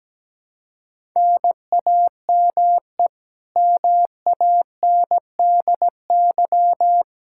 Morse code, heard as a keyed tone, about 17 wpm, 700 Hz. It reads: NAME MANDY
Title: Practice QSO - Exchange 1 - Beginning - 17wpm